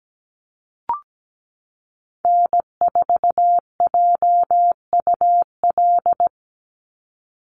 Morse code, heard as a keyed tone, about 17 wpm, 700 Hz. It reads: N4JUL
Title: USA Call Signs 17wpm